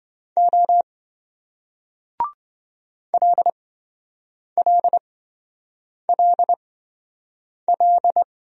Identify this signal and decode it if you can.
Morse code, heard as a keyed tone, about 27 wpm, spaced as for 7 wpm, 700 Hz. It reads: O LLLL